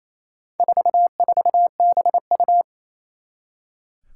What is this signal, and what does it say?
Morse code, heard as a keyed tone, about 29 wpm, 700 Hz. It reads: 44BU
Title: Four Challenging Letter-Number Combinations 28wpm